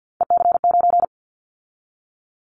Morse code, heard as a keyed tone, about 50 wpm, 700 Hz. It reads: EC9
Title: Callsign Prefixes for Top 100 DXCC - 50wpm